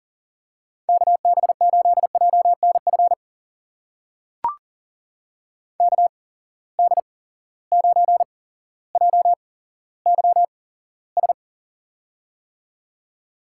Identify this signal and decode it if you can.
Morse code, heard as a keyed tone, about 37 wpm, spaced as for 17 wpm, 700 Hz. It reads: KB8JNF KD9JYS
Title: USA Call Signs - 6 Characters - 8x Character Spacing - 40wpm